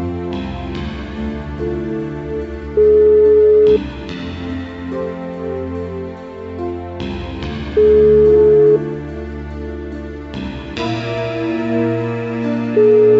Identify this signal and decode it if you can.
Ringback tone (Europe, 425 Hz).